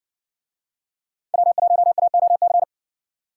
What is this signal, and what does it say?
Morse code, heard as a keyed tone, about 60 wpm, 700 Hz. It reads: W1AOC